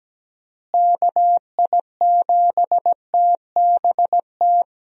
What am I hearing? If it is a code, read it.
Morse code, heard as a keyed tone, about 17 wpm, 700 Hz. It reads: KI7TBT
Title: USA Call Signs - 6 Characters - Repeated 3x - 17wpm